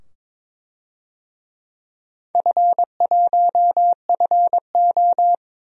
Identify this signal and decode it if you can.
Morse code, heard as a keyed tone, about 22 wpm, 700 Hz. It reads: F1FO
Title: DXCC-100 Call Signs - 4 Characters - 22wpm